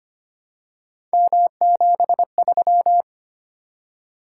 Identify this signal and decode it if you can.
Morse code, heard as a keyed tone, about 25 wpm, 700 Hz. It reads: M73
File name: Three Letter-Number Combinations - 25wpm